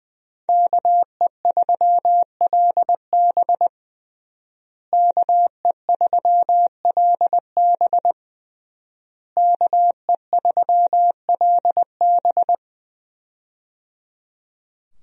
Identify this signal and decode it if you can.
Morse code, heard as a keyed tone, about 20 wpm, 700 Hz. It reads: KE3LB KE3LB KE3LB
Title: USA Call Signs - 5 Characters - Repeated 3x - 20wpm